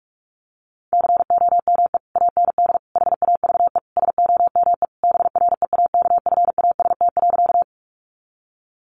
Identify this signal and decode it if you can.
Morse code, heard as a keyed tone, about 46 wpm, 700 Hz. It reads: COME AND HAVE SOME BREAKFAST.
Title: Sentences from Top 1500 Words 45wpm